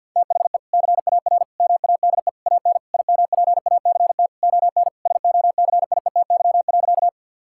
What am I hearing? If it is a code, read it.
Morse code, heard as a keyed tone, about 50 wpm, 700 Hz. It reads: THE CAR MADE AN IMPACT ON SOCIETY.
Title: Sentences from Top 1000 Words - Review Part 6 - 50wpm